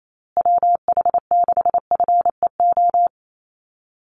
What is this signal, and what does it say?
Morse code, heard as a keyed tone, about 28 wpm, 700 Hz. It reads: WH6FEO
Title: USA Call Signs Repeated 3x 28wpm